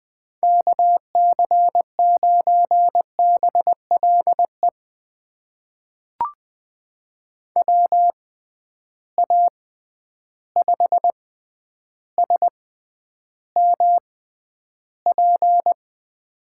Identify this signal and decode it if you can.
Morse code, heard as a keyed tone, about 20 wpm, spaced as for 10 wpm, 700 Hz. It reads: KC9BLE WA5SMP
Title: USA Call Signs - 6 Characters - 6x Character Spacing - 20wpm